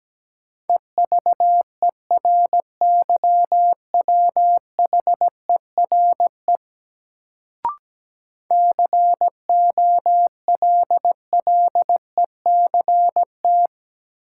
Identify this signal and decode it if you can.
Morse code, heard as a keyed tone, about 17 wpm, 700 Hz. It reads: EVERYWHERE COLLECT
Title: Sentences from Top 1600 Words - New Word Drill - 17wpm